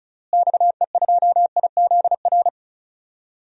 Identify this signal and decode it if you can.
Morse code, heard as a keyed tone, about 34 wpm, 700 Hz. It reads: XE2IZR
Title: DXCC-100 Call Signs - 6 Characters - Repeated 2x - 35wpm